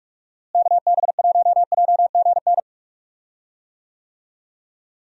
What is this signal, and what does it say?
Morse code, heard as a keyed tone, about 45 wpm, 700 Hz. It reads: KB1JGN
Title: USA Call Signs - 6 Characters - Repeated 3x - 45wpm